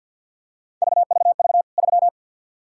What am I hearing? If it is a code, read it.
Morse code, heard as a keyed tone, about 52 wpm, 700 Hz. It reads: VVV 3
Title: K1USN SST QSO Practice 50wpm